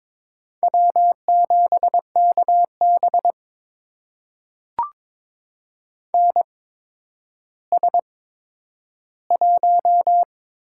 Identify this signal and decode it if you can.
Morse code, heard as a keyed tone, about 23 wpm, spaced as for 11 wpm, 700 Hz. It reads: W7KB NS1
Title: USA Call Signs - 8x Character Spacing - 22wpm